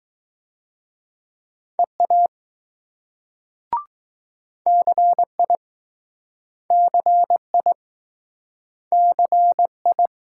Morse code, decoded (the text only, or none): EA CI CI CI